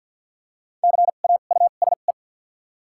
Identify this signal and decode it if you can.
Morse code, heard as a keyed tone, about 51 wpm, spaced as for 36 wpm, 700 Hz. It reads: CAUSE
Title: 2x Character Spacing - Top 200 Words - 50wpm